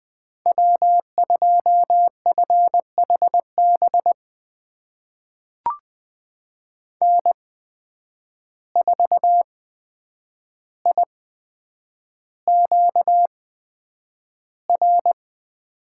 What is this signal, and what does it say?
Morse code, heard as a keyed tone, about 20 wpm, spaced as for 9 wpm, 700 Hz. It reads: W2FHB N4IQR
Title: USA Call Signs - 5 Characters - 8x Character Spacing - 20wpm